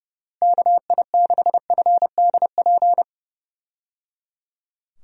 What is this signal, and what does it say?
Morse code, heard as a keyed tone, about 30 wpm, 700 Hz. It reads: KI6FDP
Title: USA Call Signs - 6 Characters - Repeated 3x - 30wpm